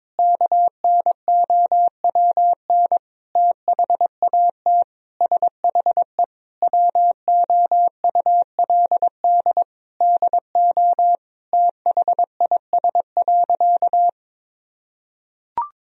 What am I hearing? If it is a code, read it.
Morse code, heard as a keyed tone, about 21 wpm, 700 Hz. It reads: KNOWN THAT SHE WOULD DO THIS.
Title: Sentences from Top 1000 Words - Review Part 3 - 22wpm